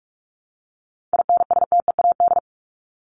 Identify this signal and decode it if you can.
Morse code, heard as a keyed tone, about 44 wpm, 700 Hz. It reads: INSTEAD
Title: Top 501-750 Words 45wpm